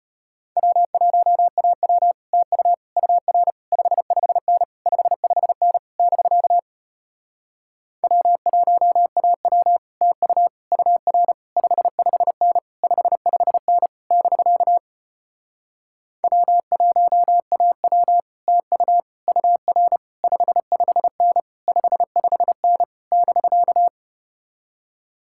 Morse code, heard as a keyed tone, about 33 wpm, 700 Hz. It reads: W1AW TU UR 55N 55N <BK> W1AW TU UR 55N 55N <BK> W1AW TU UR 55N 55N <BK>